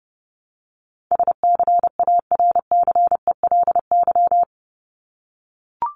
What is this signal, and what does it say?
Morse code, heard as a keyed tone, about 30 wpm, 700 Hz. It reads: SCARCELY
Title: LY Suffix - Top 100 Words with Base Words 30wpm